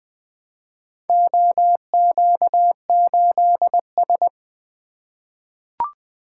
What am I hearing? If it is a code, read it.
Morse code, heard as a keyed tone, about 20 wpm, 700 Hz. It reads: OQ8S
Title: Four Letter-Number Combinations - 20wpm